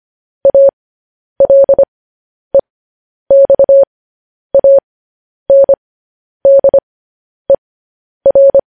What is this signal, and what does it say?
Morse code, heard as a keyed tone, about 25 wpm, spaced as for 10 wpm, 550 Hz. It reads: ALEXANDER
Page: news headlines in morse code at 10 wpm